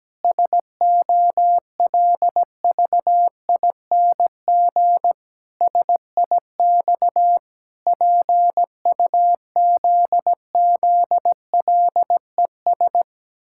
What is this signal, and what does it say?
Morse code, heard as a keyed tone, about 17 wpm, 700 Hz. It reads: SOLVING SIX PUZZLES
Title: Pangram Challenge v2 - 17wpm